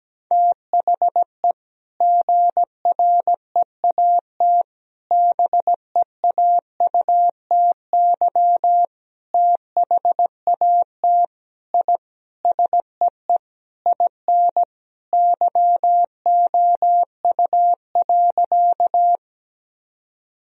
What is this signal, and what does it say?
Morse code, heard as a keyed tone, about 17 wpm, 700 Hz. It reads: THE GREAT BEAUTY THAT I SEE IN YOU.
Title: Sentences from Top 1000 Words - Review Part 2 - 17wpm